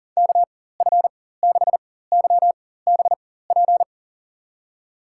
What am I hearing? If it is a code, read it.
Morse code, heard as a keyed tone, about 40 wpm, spaced as for 19 wpm, 700 Hz. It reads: KF6YBP
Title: USA Call Signs - 6 Characters - 4x Character Spacing - 40wpm